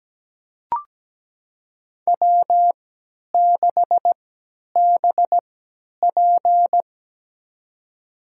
Morse code, heard as a keyed tone, about 17 wpm, spaced as for 10 wpm, 700 Hz. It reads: W6BP
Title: USA Call Signs - 3x Character Spacing - 17wpm